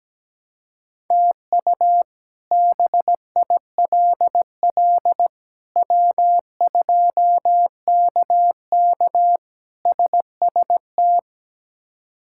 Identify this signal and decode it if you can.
Morse code, heard as a keyed tone, about 17 wpm, 700 Hz. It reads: TU BILL W2KK SST
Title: K1USN SST QSO Practice - 17wpm